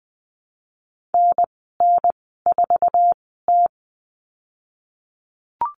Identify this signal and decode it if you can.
Morse code, heard as a keyed tone, about 20 wpm, spaced as for 15 wpm, 700 Hz. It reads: NN4T
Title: 048 Lesson 9g - Number 4 - Call Signs 20wpm